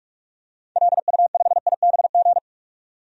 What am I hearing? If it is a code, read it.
Morse code, heard as a keyed tone, about 46 wpm, 700 Hz. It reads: LU5IBG
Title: DXCC-100 Call Signs - 6 Characters - 45wpm